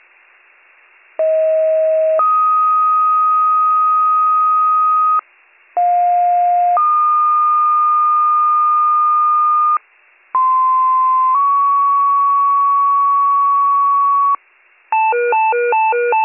Two-tone page, heard as a one-second tone, then a three-second tone, about 640 then 1250 Hz, about 710 then 1220 Hz, about 1040 then 1120 Hz. It is heard from 1.2 to 14.4 s.